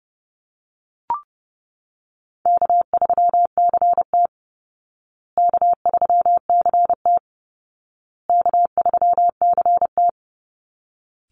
Morse code, heard as a keyed tone, about 30 wpm, 700 Hz. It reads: K3CT K3CT K3CT
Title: Easy Call Signs Repeated 3x - 30wpm